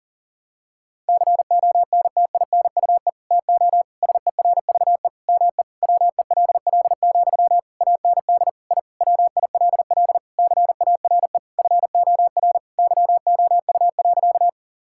Morse code, heard as a keyed tone, about 40 wpm, 700 Hz. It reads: CONTINUE TO SERVE ME WELL, AND I WILL CARE FOR YOU.
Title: Sentences from Top 1000 Words - Review Part 6 - 40wpm